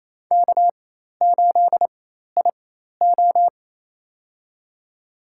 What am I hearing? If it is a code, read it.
Morse code, heard as a keyed tone, about 28 wpm, spaced as for 13 wpm, 700 Hz. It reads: K8IO
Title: USA Call Signs - 4 Characters - 4x Character Spacing - 28wpm